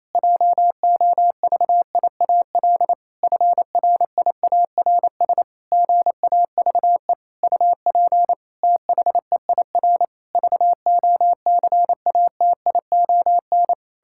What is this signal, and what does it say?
Morse code, heard as a keyed tone, about 28 wpm, 700 Hz. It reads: JOVIAL FRIARS GAVE UP THEIR VOCATION
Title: Pangram Challenge v2 - 28wpm